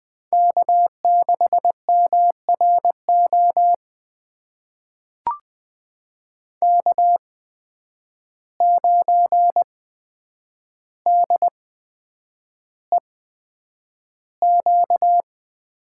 Morse code, heard as a keyed tone, about 20 wpm, spaced as for 9 wpm, 700 Hz. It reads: K6MRO K9DEQ
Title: USA Call Signs - 8x Character Spacing - 20wpm